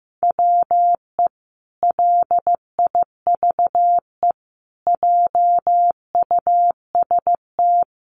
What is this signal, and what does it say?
Morse code, heard as a keyed tone, about 15 wpm, 700 Hz. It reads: WE LIVE JUST
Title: Sentences from Top 1300 Words 15wpm